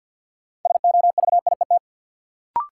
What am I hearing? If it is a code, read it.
Morse code, heard as a keyed tone, about 52 wpm, 700 Hz. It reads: SOVIET
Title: Sets of 5 Words Encore - New Word Drill - 50wpm